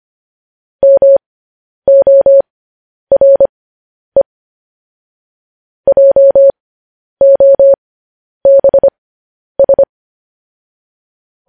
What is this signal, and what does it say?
Morse code, heard as a keyed tone, about 25 wpm, spaced as for 10 wpm, 550 Hz. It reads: MORE JOBS
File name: podcast / News headlines-1x-25x10